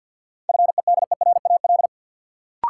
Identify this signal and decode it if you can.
Morse code, heard as a keyed tone, about 52 wpm, 700 Hz. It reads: FEDERAL
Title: Sentences from Top 1100 Words - New Word Drill Alt - 50wpm